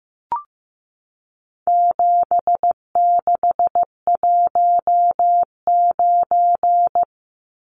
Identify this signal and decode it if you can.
Morse code, heard as a keyed tone, about 15 wpm, 700 Hz. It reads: 7619
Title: Numbers - 4-Digit 15wpm